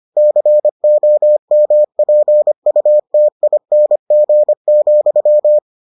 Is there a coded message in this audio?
Yes: Morse code from 0.2 to 5.6 s, heard as a keyed tone, about 25 wpm, 600 Hz. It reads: COMPUTING,